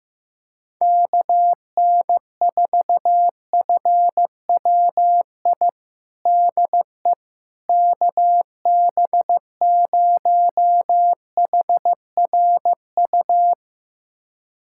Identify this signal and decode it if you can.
Morse code, heard as a keyed tone, about 15 wpm, 700 Hz. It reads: KN4FWI DE KB0HRU
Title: USA Callsign DE Callsign - 6 Characters - 15wpm